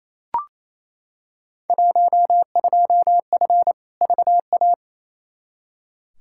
Morse code, heard as a keyed tone, about 28 wpm, 700 Hz. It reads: 12F VA
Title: Field Day Exchanges Encore - 28wpm